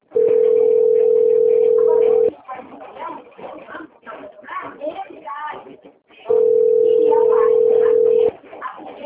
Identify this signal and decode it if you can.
Ringback tone (North America), 440 + 480 Hz, with speech.
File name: out- - - - - .22005